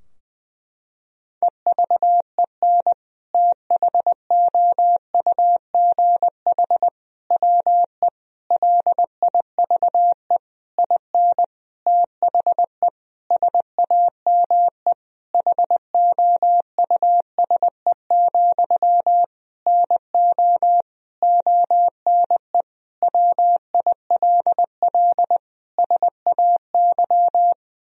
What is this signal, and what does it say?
Morse code, heard as a keyed tone, about 21 wpm, 700 Hz. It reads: EVEN THOUGH WE LIVE IN THE SAME HOUSE, NO ONE WILL SAY
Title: Sentences from Top 1000 Words - Review Part 2 - 20wpm